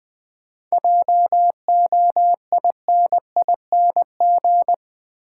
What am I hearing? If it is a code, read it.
Morse code, heard as a keyed tone, about 21 wpm, 700 Hz. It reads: JOINING
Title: Sets of 5 Words - New Word Drill - 20wpm